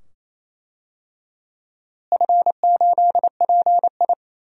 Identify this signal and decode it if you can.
Morse code, heard as a keyed tone, about 27 wpm, 700 Hz. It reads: F8PI
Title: DXCC-100 Call Signs - 4 Characters - Repeated 2x - 28wpm